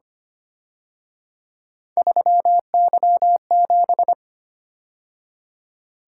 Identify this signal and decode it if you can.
Morse code, heard as a keyed tone, about 31 wpm, 700 Hz. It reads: 3Y7
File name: Speed-Racing - Three Letter-Number Combinations - 25wpm